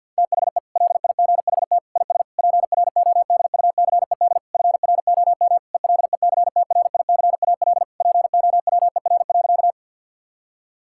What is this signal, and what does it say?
Morse code, heard as a keyed tone, about 50 wpm, 700 Hz. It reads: THE LIGHT IS PRODUCED FROM ELECTRICAL POWER.
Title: Sentences from Top 1000 Words - Review Part 6 - 50wpm